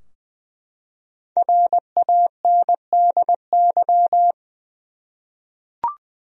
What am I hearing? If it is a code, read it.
Morse code, heard as a keyed tone, about 20 wpm, 700 Hz. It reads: RANDY
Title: USA Common Names - Max 5 Letters - 20wpm